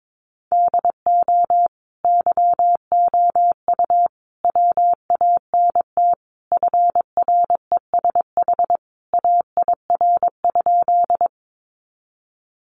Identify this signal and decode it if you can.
Morse code, heard as a keyed tone, about 22 wpm, 700 Hz. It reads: DO YOU WANT FRESH AIR?
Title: Sentences from Top 1200 Words 22wpm